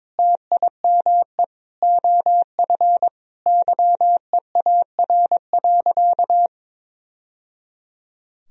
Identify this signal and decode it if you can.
Morse code, heard as a keyed tone, about 22 wpm, 700 Hz. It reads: TIME OF YEAR.
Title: Sentences from Top 1000 Words - Review Part 3 - 22wpm